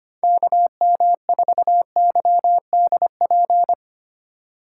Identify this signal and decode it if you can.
Morse code, heard as a keyed tone, about 25 wpm, 700 Hz. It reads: KM4YDP
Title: USA Call Signs - 6 Characters - 4x Character Spacing - 25wpm